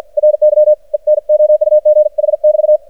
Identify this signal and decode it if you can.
Morse code, heard as a keyed tone, about 54 wpm, 610 Hz. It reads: RY ENOUGH <BT>
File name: archived / CRSnet